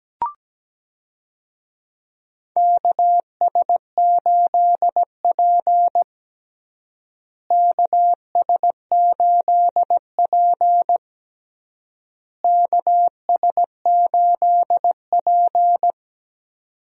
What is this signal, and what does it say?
Morse code, heard as a keyed tone, about 17 wpm, 700 Hz. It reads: KS8P KS8P KS8P
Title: USA Call Signs - 4 Characters - Repeated 3x - 17wpm